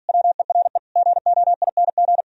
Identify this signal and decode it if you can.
Morse code, heard as a keyed tone, about 54 wpm, 700 Hz. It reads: WERE GOING